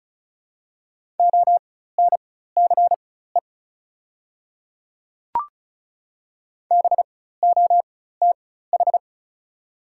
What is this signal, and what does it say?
Morse code, heard as a keyed tone, about 34 wpm, spaced as for 16 wpm, 690 Hz. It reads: ONCE BOTH
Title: 4x Character Spacing - Top 300 Words - 35wpm